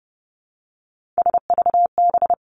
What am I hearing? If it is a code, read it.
Morse code, heard as a keyed tone, about 30 wpm, 700 Hz. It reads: SVB
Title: Three Challenging Letter-Number Combinations 30wpm